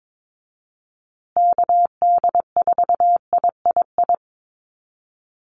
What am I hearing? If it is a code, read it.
Morse code, heard as a keyed tone, about 22 wpm, 700 Hz. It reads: KD4III
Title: USA Call Signs Repeated 2x 22wpm